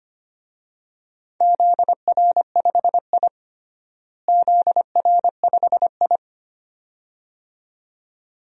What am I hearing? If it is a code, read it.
Morse code, heard as a keyed tone, about 25 wpm, 700 Hz. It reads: ZR5I ZR5I
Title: DXCC-100 Call Signs - 4 Characters - Repeated 2x - 25wpm